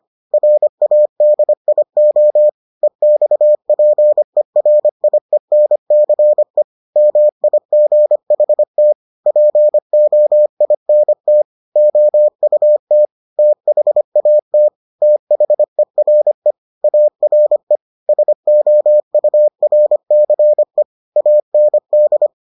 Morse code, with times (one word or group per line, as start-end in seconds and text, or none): 0.3-2.5 RADIO
2.8-6.6 EXPERIENCE
7.0-8.9 MIGHT
9.3-11.4 POINT
11.8-13.1 OUT
13.4-14.7 THAT
15.0-16.5 THERE
16.8-17.8 ARE
18.1-20.8 SOURCE
21.2-22.3 AND